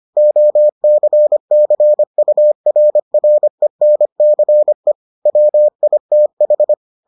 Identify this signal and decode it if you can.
Morse code, heard as a keyed tone, about 25 wpm, 600 Hz. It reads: OCCURRENCE WITH